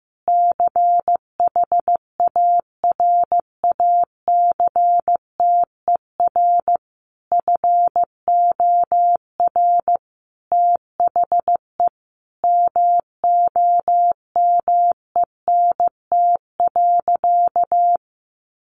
Morse code, heard as a keyed tone, about 15 wpm, 700 Hz. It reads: CHARACTER FOR THE MOMENT.
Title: Sentences from Top 1200 Words 15wpm